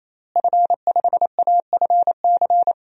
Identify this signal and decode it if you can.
Morse code, heard as a keyed tone, about 28 wpm, 700 Hz. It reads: F5AFC
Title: DXCC-100 Call Signs - 5 Characters - Repeated 3x - 28wpm